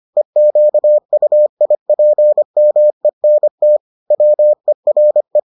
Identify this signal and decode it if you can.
Morse code, heard as a keyed tone, about 25 wpm, 600 Hz. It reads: EQUIPMENT WERE